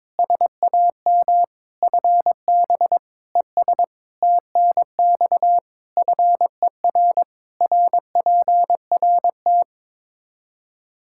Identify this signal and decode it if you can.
Morse code, heard as a keyed tone, about 21 wpm, 700 Hz. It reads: SAM FB ES TNX FER RPRT
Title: Practice QSO - Exchange 2 - Info - 22wpm